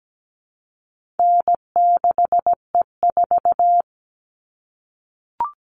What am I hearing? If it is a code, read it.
Morse code, heard as a keyed tone, about 17 wpm, 700 Hz. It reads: N6E4
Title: All Cut Numbers - 4-Digit 17wpm